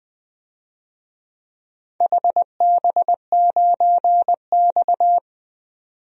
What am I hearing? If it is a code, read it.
Morse code, heard as a keyed tone, about 20 wpm, 700 Hz. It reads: HB9X
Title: DXCC-100 Call Signs - 4 Characters - 20wpm